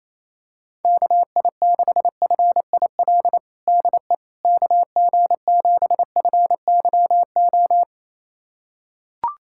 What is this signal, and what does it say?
Morse code, heard as a keyed tone, about 28 wpm, 700 Hz. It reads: KI6FIL DE KG7FYO
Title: USA Callsign DE Callsign 28wpm